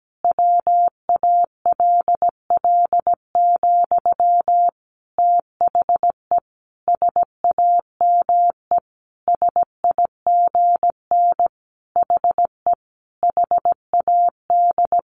Morse code, heard as a keyed tone, about 17 wpm, 700 Hz. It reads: WALL, THE SAME SIGN HE HAD
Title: A Sound Of Thunder 17wpm